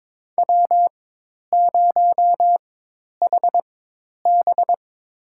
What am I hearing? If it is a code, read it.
Morse code, heard as a keyed tone, about 22 wpm, spaced as for 10 wpm, 700 Hz. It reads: W0HB
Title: USA Call Signs - 4 Characters - 4x Character Spacing - 22wpm